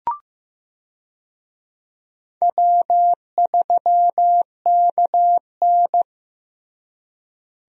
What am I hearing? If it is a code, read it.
Morse code, heard as a keyed tone, about 15 wpm, 700 Hz. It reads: W3KN
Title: USA Call Signs - 4 Characters - Repeated 2x - 15wpm